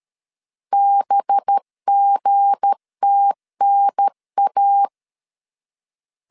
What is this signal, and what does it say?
Morse code, heard as a keyed tone, about 13 wpm, 800 Hz. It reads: BGTNA
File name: LEZIONI / 8 YJ45 / 8 Y 12wpm 2228